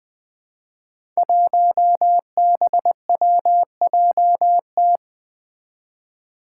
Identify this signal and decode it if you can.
Morse code, heard as a keyed tone, about 20 wpm, 700 Hz. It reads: 1BWJT
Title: Five Challenging Letter-Number Combinations 20wpm